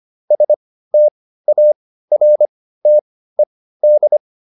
Morse code, heard as a keyed tone, about 25 wpm, spaced as for 15 wpm, 600 Hz. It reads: STARTED